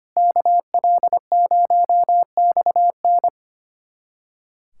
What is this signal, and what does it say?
Morse code, heard as a keyed tone, about 25 wpm, 700 Hz. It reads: KL0XN
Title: USA Callsign DE Callsign - 5 Characters - 25wpm